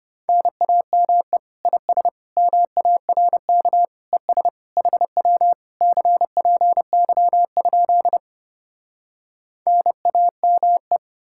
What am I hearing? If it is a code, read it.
Morse code, heard as a keyed tone, about 33 wpm, 700 Hz. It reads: NAME IS MARK ES HW CPY? NAME